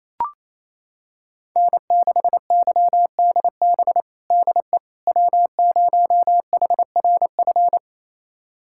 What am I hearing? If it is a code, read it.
Morse code, heard as a keyed tone, about 28 wpm, 700 Hz. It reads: N6YDB DE W0HRF
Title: USA Callsign DE Callsign - 5 Characters - 28wpm